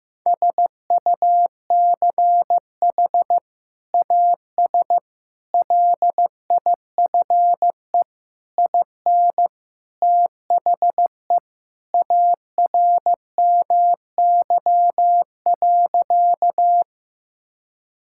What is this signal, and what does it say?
Morse code, heard as a keyed tone, about 16 wpm, 700 Hz. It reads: SUCH AS LIFE IN THE ARMY.
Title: Sentences from Top 1000 Words - Review Part 5 - 15wpm